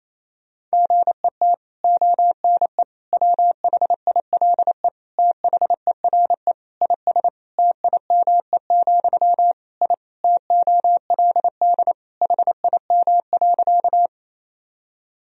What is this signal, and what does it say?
Morse code, heard as a keyed tone, about 28 wpm, 700 Hz. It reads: GET ONE WHILE THERE IS TIME, I TOLD HIM.
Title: Sentences from Top 1000 Words - Review Part 1 - 28wpm